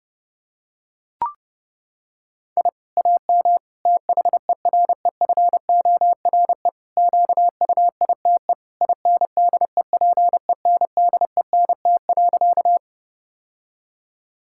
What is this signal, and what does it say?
Morse code, heard as a keyed tone, about 32 wpm, 700 Hz. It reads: I AM THEREFORE QUITE INDEPENDENT.
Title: Sentences from Top 1000 Words - Review Part 6 - 30wpm